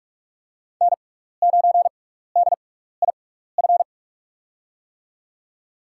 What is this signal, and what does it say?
Morse code, heard as a keyed tone, about 41 wpm, spaced as for 15 wpm, 700 Hz. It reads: N9DIF
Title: USA Call Signs - 6x Character Spacing - 45wpm